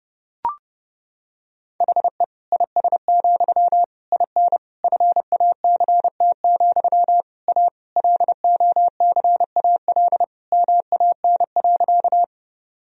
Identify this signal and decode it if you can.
Morse code, heard as a keyed tone, about 30 wpm, 700 Hz. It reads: HE IS, IN FACT, A LOCAL MAN.
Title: Sentences from Top 1000 Words - Review Part 3 - 30wpm